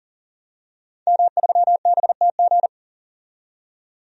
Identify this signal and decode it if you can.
Morse code, heard as a keyed tone, about 40 wpm, 700 Hz. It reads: M3BTG
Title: DXCC-100 Call Signs - 5 Characters - 40wpm